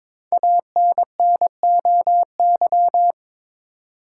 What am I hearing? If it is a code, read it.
Morse code, heard as a keyed tone, about 22 wpm, 700 Hz. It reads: ANNOY
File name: 5 Letter Words - First 3 Letters Given - 22wpm